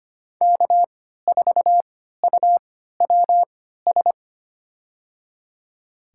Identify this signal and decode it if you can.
Morse code, heard as a keyed tone, about 25 wpm, spaced as for 14 wpm, 700 Hz. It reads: K4UWS
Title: USA Call Signs - 5 Characters - 3x Character Spacing - 25wpm